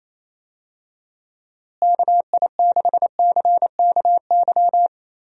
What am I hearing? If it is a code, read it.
Morse code, heard as a keyed tone, about 28 wpm, 700 Hz. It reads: KI6CKY
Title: USA Call Signs - 6 Characters - Repeated 3x - 28wpm